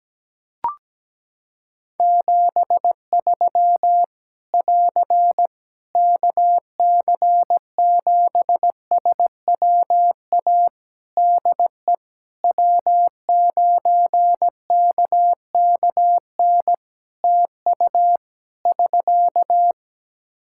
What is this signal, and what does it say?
Morse code, heard as a keyed tone, about 17 wpm, 700 Hz. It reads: 73 <AR> KC7SWA DE W9KKN TU <SK>
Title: Practice QSO - Exchange 4 - Ending - 17wpm